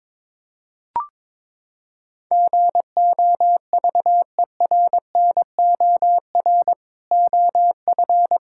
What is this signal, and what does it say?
Morse code, heard as a keyed tone, about 22 wpm, 700 Hz. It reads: GOVERNOR OF
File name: practice / 22wpm / Sentences with 3-Word Phrases - 22wpm